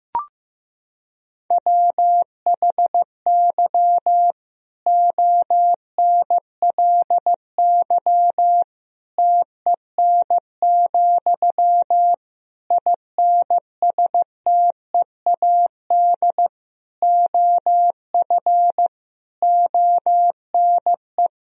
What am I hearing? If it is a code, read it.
Morse code, heard as a keyed tone, about 15 wpm, 700 Hz. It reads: WHY ONLY TEN, INSTEAD OF ONE
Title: Sentences from Top 1000 Words - Review Part 4 - 15wpm